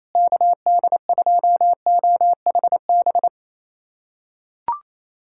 Morse code, heard as a keyed tone, about 28 wpm, 700 Hz. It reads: KD2OHB
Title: USA Callsign DE Callsign - 6 Characters - 28wpm